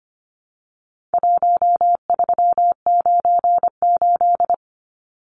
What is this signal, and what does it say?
Morse code, heard as a keyed tone, about 25 wpm, 700 Hz. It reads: 1398
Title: Numbers - Contest Style - 4-Digit 25wpm